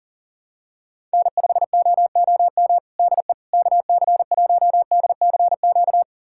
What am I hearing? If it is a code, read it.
Morse code, heard as a keyed tone, about 37 wpm, 700 Hz. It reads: N5OOM DE KC1DCQ
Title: USA Callsign DE Callsign 40wpm